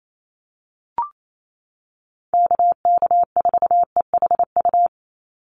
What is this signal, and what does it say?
Morse code, heard as a keyed tone, about 28 wpm, 700 Hz. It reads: KK4EHU
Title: USA Call Signs 28wpm